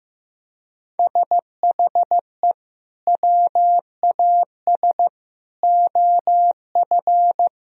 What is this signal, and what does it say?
Morse code, heard as a keyed tone, about 16 wpm, 700 Hz. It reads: SHE WAS OF
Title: Sentences from Top 1000 Words - Review Part 4 - 15wpm